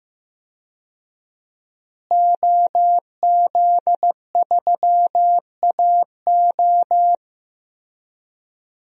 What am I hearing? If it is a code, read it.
Morse code, heard as a keyed tone, about 15 wpm, 700 Hz. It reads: OZ3AO
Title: DXCC-100 Call Signs - 5 Characters - Repeated 2x - 15wpm